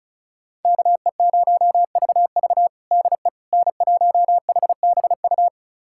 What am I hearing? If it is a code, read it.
Morse code, heard as a keyed tone, about 36 wpm, 700 Hz. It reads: KE0VV DE N1HBU
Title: USA Callsign DE Callsign - 5 Characters - 35wpm